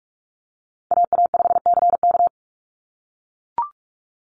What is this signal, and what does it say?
Morse code, heard as a keyed tone, about 44 wpm, 690 Hz. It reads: AA5CK
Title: NA SOTA Chaser Callsigns 45wpm